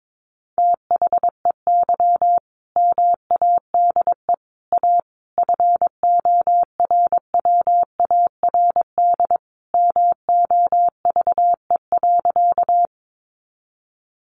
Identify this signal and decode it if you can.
Morse code, heard as a keyed tone, about 22 wpm, 700 Hz. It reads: THEY MADE A FORWARD MOVE.
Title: Sentences from Top 1300 Words 22wpm